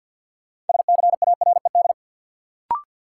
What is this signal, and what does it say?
Morse code, heard as a keyed tone, about 53 wpm, 700 Hz. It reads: SCARED
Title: Sentences from Top 700 Words - New Word Drill - 50wpm